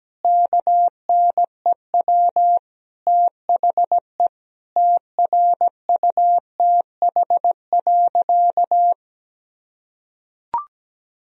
Morse code, heard as a keyed tone, about 17 wpm, 700 Hz. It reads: KNEW THE TRUTH.
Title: Sentences from Top 1000 Words - Review Part 1 - 17wpm